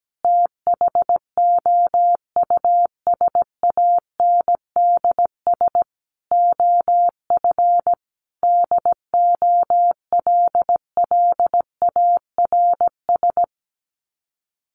Morse code, heard as a keyed tone, about 17 wpm, 700 Hz. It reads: THOUSANDS OF DOLLARS
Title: Sets of 5 Words Encore2 17wpm